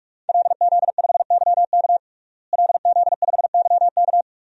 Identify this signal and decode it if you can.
Morse code, heard as a keyed tone, about 46 wpm, 700 Hz. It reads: LZ5YK LZ5YK
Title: DXCC-100 Call Signs - 5 Characters - Repeated 2x - 45wpm